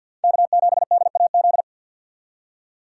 Morse code, heard as a keyed tone, about 52 wpm, 700 Hz. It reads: K7DAZ
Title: USA Call Signs - 5 Characters - 3x Character Spacing - 50wpm